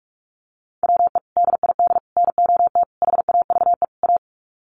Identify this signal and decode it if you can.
Morse code, heard as a keyed tone, about 44 wpm, 700 Hz. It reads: WE DID NOT HAVE A
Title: Sets of 5 Words Encore2 45wpm